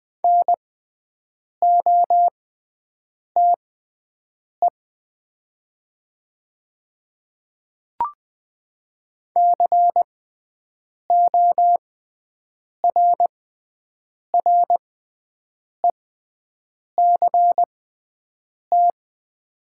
Morse code, heard as a keyed tone, about 20 wpm, spaced as for 7 wpm, 700 Hz. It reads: NOTE CORRECT
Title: 6x Character Spacing - Top 500 Words - 20wpm